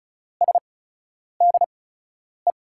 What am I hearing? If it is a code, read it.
Morse code, heard as a keyed tone, about 36 wpm, spaced as for 10 wpm, 700 Hz. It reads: SDE
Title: USA Call Signs - 6 Characters - 8x Character Spacing - 35wpm